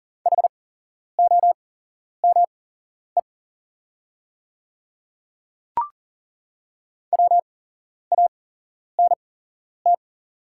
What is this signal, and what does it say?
Morse code, heard as a keyed tone, about 40 wpm, spaced as for 11 wpm, 690 Hz. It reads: HOME WANT
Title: 8x Character Spacing - Top 300 Words - 40wpm